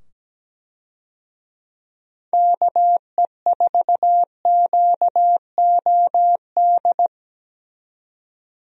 Morse code, heard as a keyed tone, about 17 wpm, 700 Hz. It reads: KE4QOD
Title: USA Call Signs - 6 Characters - 17wpm